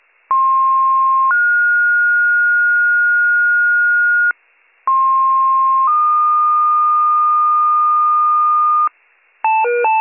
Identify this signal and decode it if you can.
Two-tone page, heard as a one-second tone, then a three-second tone, about 1060 then 1430 Hz, about 1060 then 1220 Hz, with noise. It is heard from 0.3 to 8.9 s.